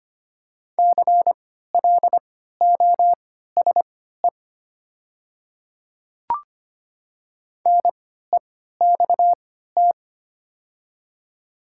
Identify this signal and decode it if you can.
Morse code, heard as a keyed tone, about 25 wpm, spaced as for 14 wpm, 690 Hz. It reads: CLOSE NEXT
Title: 3x Character Spacing - Top 300 Words - 25wpm